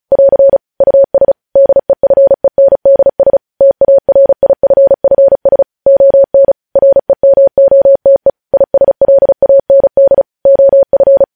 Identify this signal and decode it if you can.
Morse code, heard as a keyed tone, about 36 wpm, 550 Hz. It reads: <AR> US DEFENDS TARIFFS ON REMOTE ISLAND OF
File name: podcast / News headlines-1x-35x35